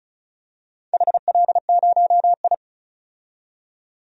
Morse code, heard as a keyed tone, about 34 wpm, 700 Hz. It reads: HL0I
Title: DXCC-100 Call Signs - 4 Characters - 35wpm